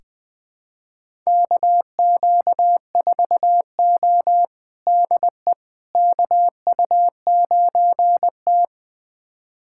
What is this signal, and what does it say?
Morse code, heard as a keyed tone, about 20 wpm, 700 Hz. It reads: KQ4O DE KU9T
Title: USA Callsign DE Callsign - 4 Characters - 20wpm